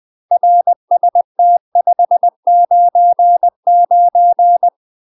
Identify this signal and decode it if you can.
Morse code, heard as a keyed tone, about 20 wpm, 700 Hz. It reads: RST 599
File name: UK MASTER QSOs TRAINING / 20WPM / UK MASTER QSOs 0031